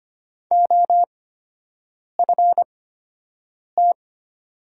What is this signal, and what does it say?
Morse code, heard as a keyed tone, about 25 wpm, spaced as for 7 wpm, 700 Hz. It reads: OFT
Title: 8x Character Spacing - Top 300 Words - 25wpm